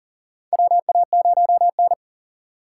Morse code, heard as a keyed tone, about 40 wpm, 700 Hz. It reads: WA0N
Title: USA Call Signs - 4 Characters - 2x Character Spacing - 40wpm